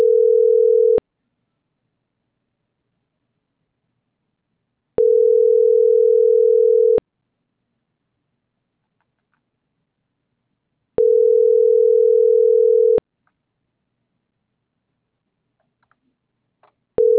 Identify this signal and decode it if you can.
Ringback tone (North America), 440 + 480 Hz.